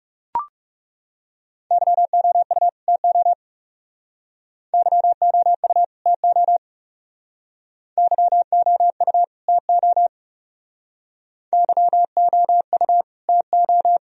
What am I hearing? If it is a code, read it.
Morse code, heard as a keyed tone, about 39 wpm, 700 Hz. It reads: YOU TO YOU TO YOU TO YOU TO